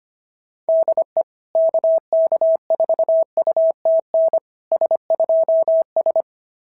Morse code, heard as a keyed tone, about 25 wpm, 690 Hz. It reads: DE KK4UTN S2S
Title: SOTA QSO Practice 25wpm